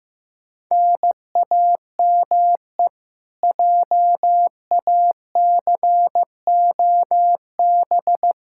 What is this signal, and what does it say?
Morse code, heard as a keyed tone, about 15 wpm, 700 Hz. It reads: NAME JACOB
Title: K1USN SST QSO Practice 15wpm